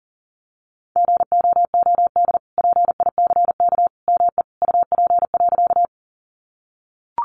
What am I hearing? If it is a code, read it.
Morse code, heard as a keyed tone, about 40 wpm, 700 Hz. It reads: GOOD PICK ME UP.
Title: Sentences from Top 1000 Words 40wpm